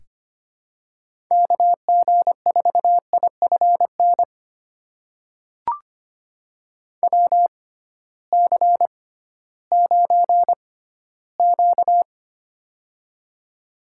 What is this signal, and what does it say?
Morse code, heard as a keyed tone, about 25 wpm, spaced as for 14 wpm, 700 Hz. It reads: KG4IFN WC9Q
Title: USA Call Signs - 6x Character Spacing - 25wpm